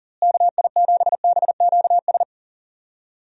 Morse code, heard as a keyed tone, about 38 wpm, 700 Hz. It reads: KI7BQS
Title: USA Call Signs - 6 Characters - Repeated 3x - 40wpm